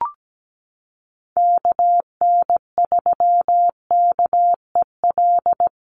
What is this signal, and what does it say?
Morse code, heard as a keyed tone, about 17 wpm, 700 Hz. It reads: KN3KEL
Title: USA Call Signs Repeated 3x 17wpm